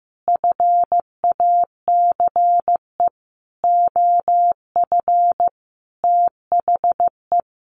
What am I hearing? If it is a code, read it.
Morse code, heard as a keyed tone, about 15 wpm, 700 Hz. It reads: FACE OF THE
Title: Sets of 5 Words 15wpm